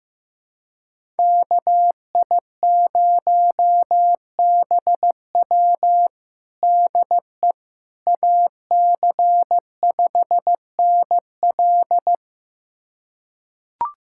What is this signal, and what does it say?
Morse code, heard as a keyed tone, about 15 wpm, 700 Hz. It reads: KI0BW DE AC5NL
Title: USA Callsign DE Callsign - 5 Characters - 15wpm